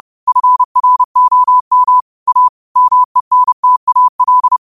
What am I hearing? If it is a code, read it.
Morse code, heard as a keyed tone, about 30 wpm, 1000 Hz. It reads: FROM A MENTAL
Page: Quotes for Mon, 28 Apr 2025 in Morse Code at 30 words per minute.